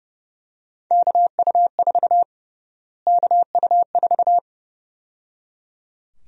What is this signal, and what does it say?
Morse code, heard as a keyed tone, about 29 wpm, 700 Hz. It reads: KU4 KU4
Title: USA Call Sign Prefixes - Repeated 2x - 30wpm